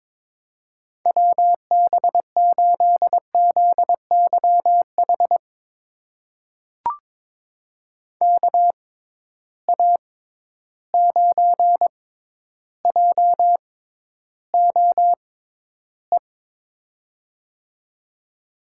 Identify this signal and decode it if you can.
Morse code, heard as a keyed tone, about 22 wpm, spaced as for 11 wpm, 700 Hz. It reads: WB8ZYH KA9JOE
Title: USA Call Signs - 6 Characters - 6x Character Spacing - 22wpm